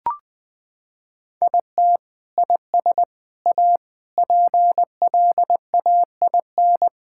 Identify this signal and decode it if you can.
Morse code, heard as a keyed tone, about 21 wpm, 700 Hz. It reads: IT IS A PLAIN
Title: Sentences from Top 1000 Words - Review Part 5 - 20wpm